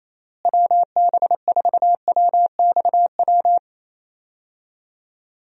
Morse code, heard as a keyed tone, about 28 wpm, 700 Hz. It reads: WB4WXW
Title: USA Call Signs - 6 Characters - Repeated 2x - 28wpm